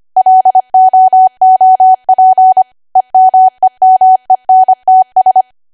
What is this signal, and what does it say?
Morse code, heard as a keyed tone, about 25 wpm, 750 Hz. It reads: LOOP EMEMENTS
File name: Archive / 25 WPM / 25WPM